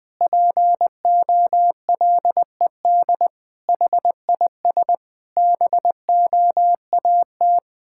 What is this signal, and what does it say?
Morse code, heard as a keyed tone, about 20 wpm, 700 Hz. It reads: POLED HIS BOAT
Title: Pangram Challenge v2 - 20wpm